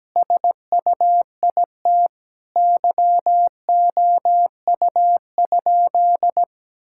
Morse code, heard as a keyed tone, about 17 wpm, 700 Hz. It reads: SUIT YOU?
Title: Sentences from Top 1600 Words 17wpm